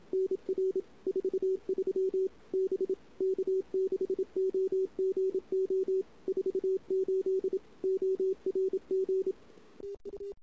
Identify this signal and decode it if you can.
Morse code, heard as a keyed tone, about 29 wpm, 410 Hz. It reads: NR 43 B K6OGO 48 ORG TU
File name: CW / 2008-ss-cw-k5tr-op-n5rz -23:00